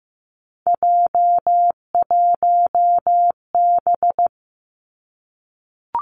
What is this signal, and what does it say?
Morse code, heard as a keyed tone, about 15 wpm, 700 Hz. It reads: J1B
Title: Three Challenging Letter-Number Combinations 15wpm